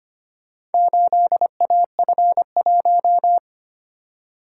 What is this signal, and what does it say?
Morse code, heard as a keyed tone, about 25 wpm, 700 Hz. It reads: 8AF1
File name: practice / 25wpm / Four Letter-Number Combinations - 25wpm